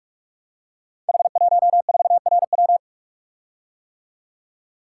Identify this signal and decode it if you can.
Morse code, heard as a keyed tone, about 44 wpm, 700 Hz. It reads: H14RW
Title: Code Groups - 5 Letter:Numbers - 45wpm